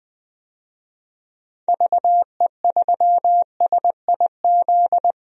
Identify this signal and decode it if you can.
Morse code, heard as a keyed tone, about 20 wpm, 700 Hz. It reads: VE3SIZ
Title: DXCC-100 Call Signs - 6 Characters - 20wpm